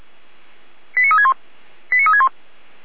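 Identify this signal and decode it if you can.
Five-tone selective call (ZVEI), heard as quick runs of five tones, tones of about 70 ms. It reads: 89361 89361
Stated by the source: ZVEI five-tone call 89356 received by SDR.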